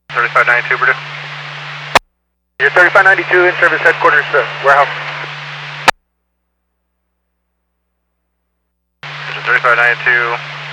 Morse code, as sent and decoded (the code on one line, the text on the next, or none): none